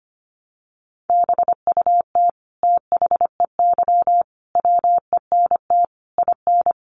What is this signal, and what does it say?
Morse code, heard as a keyed tone, about 25 wpm, 700 Hz. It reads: BUT THEY WENT IN
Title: Sentences from Top 1300 Words 25wpm